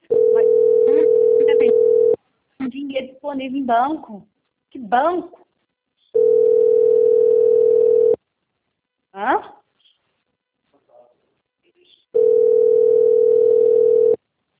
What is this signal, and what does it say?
Ringback tone (North America), 440 + 480 Hz, with speech.
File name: out- - - - - .7998